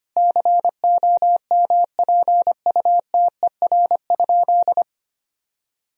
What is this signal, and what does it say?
Morse code, heard as a keyed tone, about 25 wpm, 700 Hz. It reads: COMPUTER?
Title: Sentences from Top 1000 Words - Review Part 3 - 25wpm